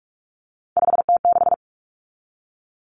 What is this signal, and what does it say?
Morse code, heard as a keyed tone, about 44 wpm, 700 Hz. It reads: 5T6
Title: Common Cut Numbers - 3-Digit 45wpm